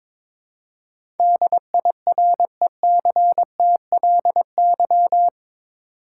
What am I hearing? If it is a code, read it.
Morse code, heard as a keyed tone, about 21 wpm, 700 Hz. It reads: DIRECTLY
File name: practice / 22wpm / Eight Letter Words - 22wpm